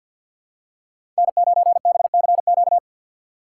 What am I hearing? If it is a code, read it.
Morse code, heard as a keyed tone, about 52 wpm, 700 Hz. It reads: N9BCQ
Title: USA Call Signs - 5 Characters - 2x Character Spacing - 50wpm